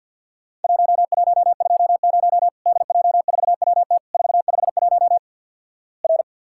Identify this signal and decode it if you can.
Morse code, heard as a keyed tone, about 50 wpm, 700 Hz. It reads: 1120 DJ4WT 451 R
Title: Worked All Europe DX Contest - QTC Exchange - 50wpm